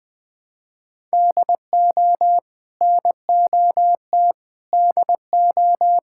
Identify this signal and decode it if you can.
Morse code, heard as a keyed tone, about 20 wpm, 700 Hz. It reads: DO NOT DO
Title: Sentences from Top 1000 Words - Review Part 6 - 20wpm